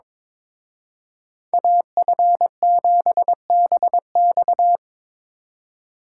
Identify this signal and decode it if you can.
Morse code, heard as a keyed tone, about 26 wpm, 700 Hz. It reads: AF7BX
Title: Speed Racing - USA 2x2 Call Signs 22wpm